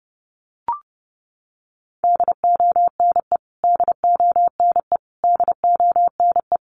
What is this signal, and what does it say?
Morse code, heard as a keyed tone, about 30 wpm, 700 Hz. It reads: DONE DONE DONE
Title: Top 500 Words Repeated 30wpm